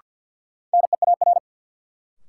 Morse code, heard as a keyed tone, about 48 wpm, 700 Hz. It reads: NEAR
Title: Sets of 5 Words Encore - New Word Drill - 50wpm